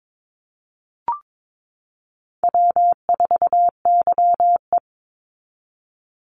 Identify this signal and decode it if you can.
Morse code, heard as a keyed tone, about 22 wpm, 690 Hz. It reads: W4YE
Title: Call Signs Repeated 2x - 22wpm